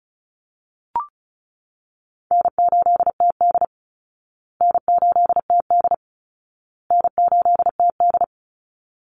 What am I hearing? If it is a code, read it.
Morse code, heard as a keyed tone, about 34 wpm, 700 Hz. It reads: N8TD N8TD N8TD
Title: Easy Call Signs Repeated 3x - 35wpm